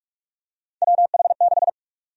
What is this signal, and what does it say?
Morse code, heard as a keyed tone, about 47 wpm, 700 Hz. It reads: WH6
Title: USA Call Sign Prefixes - 45wpm